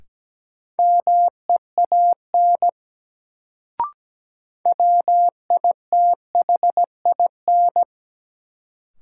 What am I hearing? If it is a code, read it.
Morse code, heard as a keyed tone, about 17 wpm, 700 Hz. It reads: MEAN WITHIN
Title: Sets of 2 Words - New Word Drill - 17wpm